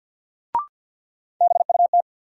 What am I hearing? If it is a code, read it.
Morse code, heard as a keyed tone, about 50 wpm, 700 Hz. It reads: BUT
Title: Sets of 2 Words - New Word Drill - 50wpm